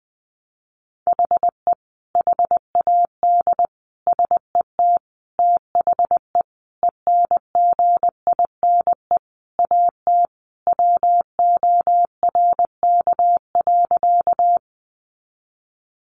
Morse code, heard as a keyed tone, about 20 wpm, 700 Hz. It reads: HE HAD SET THE ENGINE AT WORK.
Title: Sentences from Top 1500 Words 20wpm